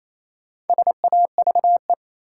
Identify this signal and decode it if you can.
Morse code, heard as a keyed tone, about 28 wpm, 700 Hz. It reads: SAVE
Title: Sentences from Top 1100 Words - New Word Drill - 28wpm